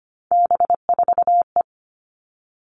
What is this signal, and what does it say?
Morse code, heard as a keyed tone, about 25 wpm, 700 Hz. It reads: B4E
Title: All Cut Numbers - 3-Digit 25wpm